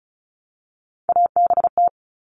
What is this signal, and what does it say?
Morse code, heard as a keyed tone, about 34 wpm, 700 Hz. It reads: ABT
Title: Abbreviations Repeated 3x - 35wpm